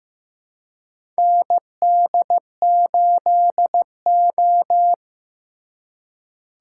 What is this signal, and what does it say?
Morse code, heard as a keyed tone, about 15 wpm, 700 Hz. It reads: ND8O
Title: USA Call Signs - 2x Character Spacing - 15wpm